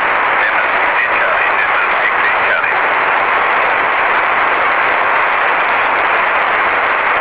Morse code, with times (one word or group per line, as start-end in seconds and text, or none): none